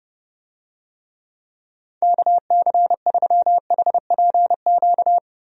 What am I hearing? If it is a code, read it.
Morse code, heard as a keyed tone, about 31 wpm, 700 Hz. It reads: KC3HPQ
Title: USA Call Signs - 6 Characters - 30wpm